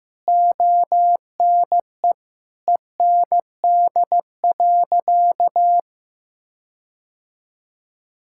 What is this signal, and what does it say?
Morse code, heard as a keyed tone, about 15 wpm, 700 Hz. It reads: ONE END.
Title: Sentences from Top 1600 Words 15wpm